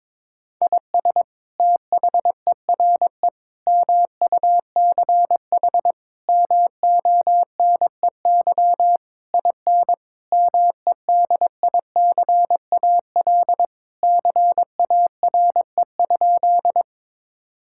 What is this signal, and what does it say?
Morse code, heard as a keyed tone, about 22 wpm, 700 Hz. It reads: IS THERE MUCH MONEY IN MEDICAL CARE?
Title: Sentences from Top 1000 Words - Review Part 5 - 22wpm